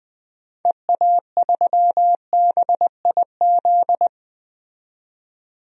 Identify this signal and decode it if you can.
Morse code, heard as a keyed tone, about 20 wpm, 700 Hz. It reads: EA3BIZ
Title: DXCC-100 Call Signs - 6 Characters - 20wpm